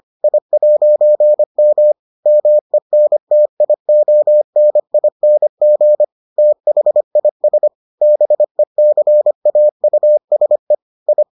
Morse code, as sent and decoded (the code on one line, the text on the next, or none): .. .----. -- / -- . -. - .. --- -. .. -. --. / - .... .. ... / -... -.-. .- ..- ... / ..
I'M MENTIONING THIS BECAUSE I